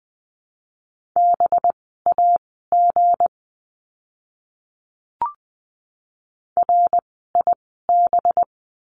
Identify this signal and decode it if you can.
Morse code, heard as a keyed tone, about 20 wpm, spaced as for 14 wpm, 700 Hz. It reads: BAG RIB
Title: 185 Lesson 30f - Letter b - Words 20wpm